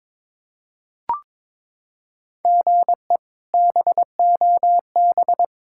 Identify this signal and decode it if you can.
Morse code, heard as a keyed tone, about 22 wpm, 700 Hz. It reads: GE BOB
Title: K1USN SST QSO Practice - 22wpm